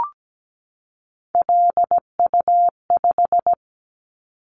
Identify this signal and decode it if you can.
Morse code, heard as a keyed tone, about 17 wpm, 700 Hz. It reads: LU5
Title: Callsign Prefixes for Top 100 DXCC - 17wpm